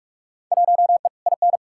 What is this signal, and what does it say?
Morse code, heard as a keyed tone, about 46 wpm, 700 Hz. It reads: 1E IN
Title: Field Day Exchanges - 45wpm